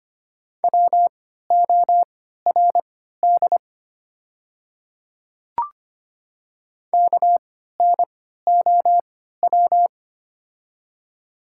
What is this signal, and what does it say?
Morse code, heard as a keyed tone, about 25 wpm, spaced as for 14 wpm, 700 Hz. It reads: WORD KNOW
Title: 3x Character Spacing - Top 200 Words - 25wpm